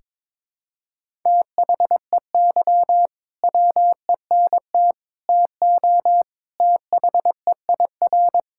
Morse code, heard as a keyed tone, about 22 wpm, 700 Hz. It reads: THEY WENT TO THEIR
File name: practice / 22wpm / Sentences with 2-Word Phrases - 22wpm